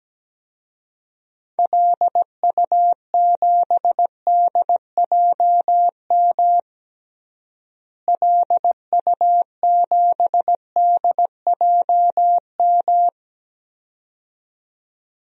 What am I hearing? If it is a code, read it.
Morse code, heard as a keyed tone, about 17 wpm, 700 Hz. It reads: LU7DJM LU7DJM
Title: DXCC-100 Call Signs - 6 Characters - Repeated 2x - 17wpm